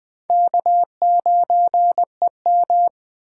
Morse code, heard as a keyed tone, about 20 wpm, 700 Hz. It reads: K9EM